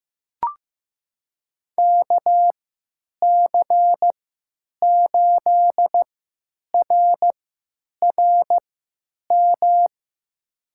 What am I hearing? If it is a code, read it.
Morse code, heard as a keyed tone, about 15 wpm, spaced as for 9 wpm, 700 Hz. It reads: KC8RRM
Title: USA Call Signs - 6 Characters - 3x Character Spacing - 15wpm